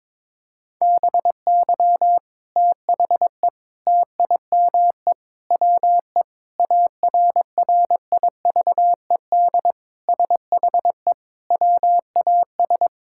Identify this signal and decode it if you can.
Morse code, heard as a keyed tone, about 22 wpm, 700 Hz. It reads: BY THE TIME WE ARRIVED SHE WAS
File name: practice / 22wpm / Sentences with 3-Word Phrases - 22wpm